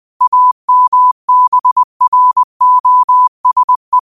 Morse code, heard as a keyed tone, about 20 wpm, 1000 Hz. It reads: AMBROSE
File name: qotdfiles / 20wpm